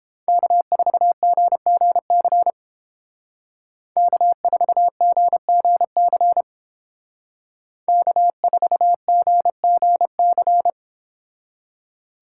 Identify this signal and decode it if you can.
Morse code, heard as a keyed tone, about 31 wpm, 700 Hz. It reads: K4GGC K4GGC K4GGC